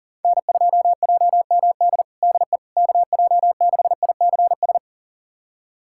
Morse code, heard as a keyed tone, about 40 wpm, 700 Hz. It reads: N2JMD DE KJ6ICS
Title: USA Callsign DE Callsign 40wpm